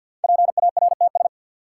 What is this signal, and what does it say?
Morse code, heard as a keyed tone, about 53 wpm, 700 Hz. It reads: PARTS
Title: Sentences from Top 1000 Words - Review Part 6 - New Word Drill - 50wpm